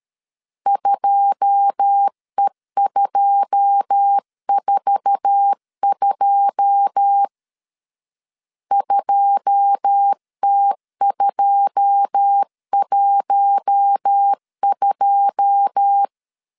Morse code, heard as a keyed tone, about 13 wpm, 800 Hz. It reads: 2E242 2T212
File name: LEZIONI / 9 2 12wpm 2301